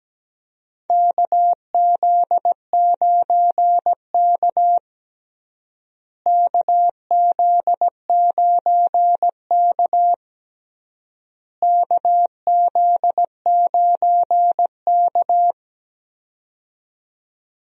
Morse code, heard as a keyed tone, about 17 wpm, 700 Hz. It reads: KZ9K KZ9K KZ9K
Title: USA Call Signs - 4 Characters - Repeated 3x - 17wpm